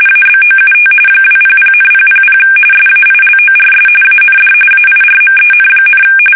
RTTY, about 45.45 baud. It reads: ABOUT 200,000 SAILORS AND AIRMEN,
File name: fsk850 / battleshipnj